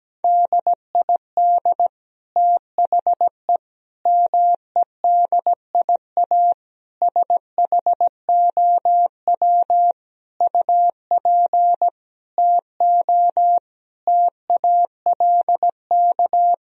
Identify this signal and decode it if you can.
Morse code, heard as a keyed tone, about 17 wpm, 700 Hz. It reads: DID THE MEDIA SHOW UP TO TALK
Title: Sentences from Top 1000 Words - Review Part 4 - 17wpm